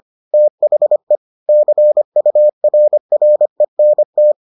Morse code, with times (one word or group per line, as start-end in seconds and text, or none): 0.3-1.2 THE
1.5-4.3 CURRENT